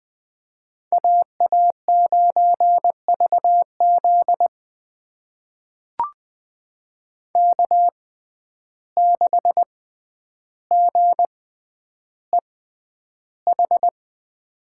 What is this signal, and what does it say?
Morse code, heard as a keyed tone, about 20 wpm, spaced as for 10 wpm, 700 Hz. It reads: AA9VZ K6GEH
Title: USA Call Signs - 5 Characters - 6x Character Spacing - 20wpm